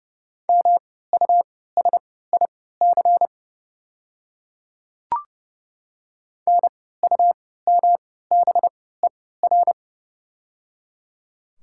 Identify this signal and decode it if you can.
Morse code, heard as a keyed tone, about 30 wpm, spaced as for 17 wpm, 700 Hz. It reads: MUSIC NUMBER
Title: 3x Character Spacing - Top 300 Words - 30wpm